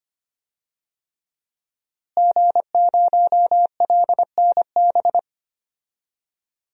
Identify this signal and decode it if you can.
Morse code, heard as a keyed tone, about 25 wpm, 700 Hz. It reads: G0LNB
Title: DXCC-100 Call Signs - 5 Characters - Repeated 3x - 25wpm